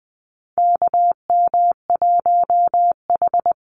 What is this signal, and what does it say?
Morse code, heard as a keyed tone, about 20 wpm, 700 Hz. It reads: KM1H
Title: USA Call Signs Repeated 3x 20wpm